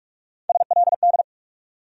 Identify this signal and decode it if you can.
Morse code, heard as a keyed tone, about 46 wpm, 700 Hz. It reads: SLD
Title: USA Call Sign Postfixes - 3 Letters 45wpm